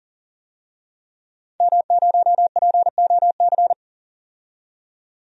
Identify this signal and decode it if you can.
Morse code, heard as a keyed tone, about 40 wpm, 700 Hz. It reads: M0POC
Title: DXCC-100 Call Signs - 5 Characters - 40wpm